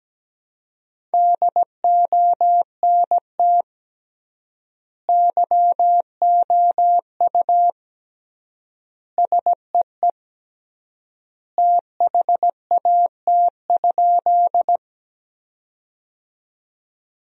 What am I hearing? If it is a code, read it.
Morse code, heard as a keyed tone, about 17 wpm, 700 Hz. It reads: DONT YOU SEE THAT?
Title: 3x Word-Spacing - Sentences from Top 300 Words - 17wpm